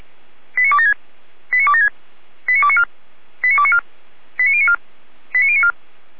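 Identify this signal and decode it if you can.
Five-tone selective call (ZVEI), heard as quick runs of five tones, tones of about 80 ms. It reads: 89276 89276 89294 89294 89094 89094
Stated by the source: ZVEI five-tone selective call 89265 received on SDR.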